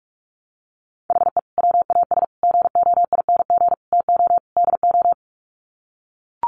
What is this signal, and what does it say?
Morse code, heard as a keyed tone, about 44 wpm, 700 Hz. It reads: HE WAS GOING TO DO
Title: Sets of 5 Words Encore2 45wpm